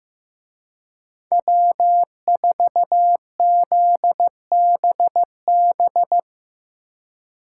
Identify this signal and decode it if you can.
Morse code, heard as a keyed tone, about 17 wpm, 700 Hz. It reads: W4ZBB
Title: Speed Racing - USA 1x3 Call Signs 15wpm